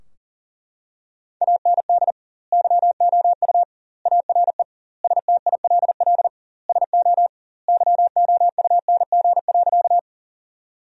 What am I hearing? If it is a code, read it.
Morse code, heard as a keyed tone, about 37 wpm, 700 Hz. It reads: AND YOU ARE STILL SO YOUNG.
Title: 2x Word-Spacing - Sentences from Top 400 Words - 40wpm